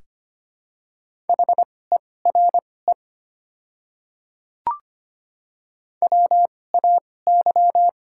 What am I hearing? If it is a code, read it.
Morse code, heard as a keyed tone, about 25 wpm, spaced as for 18 wpm, 690 Hz. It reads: HERE WAY
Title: 2x Character Spacing - Top 200 Words - 25wpm